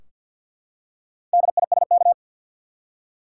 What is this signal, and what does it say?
Morse code, heard as a keyed tone, about 53 wpm, 700 Hz. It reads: DISK
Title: Four Letter Words - 50wpm